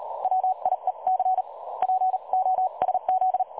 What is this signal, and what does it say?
Morse code, heard as a keyed tone, about 34 wpm, 750 Hz. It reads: WIEC POSZ
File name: dlazaawansowanych / nagraniazpasma / websdr recording 3565.0kHz